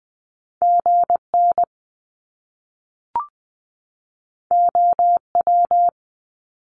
Morse code, heard as a keyed tone, about 20 wpm, 700 Hz. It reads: GN OW
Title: Top Two Letter Combinations in Words 20wpm